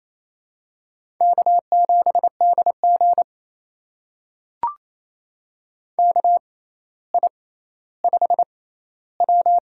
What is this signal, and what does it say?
Morse code, heard as a keyed tone, about 28 wpm, spaced as for 14 wpm, 700 Hz. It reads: K7DG KI5W
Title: USA Call Signs - 4 Characters - 6x Character Spacing - 28wpm